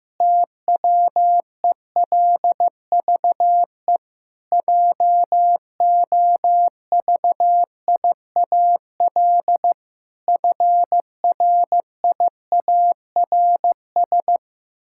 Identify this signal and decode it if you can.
Morse code, heard as a keyed tone, about 15 wpm, 700 Hz. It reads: TWELVE JOVIAL FRIARS
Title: Pangram Challenge v2 - 15wpm